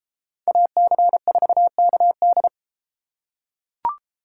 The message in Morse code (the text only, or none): AC4KD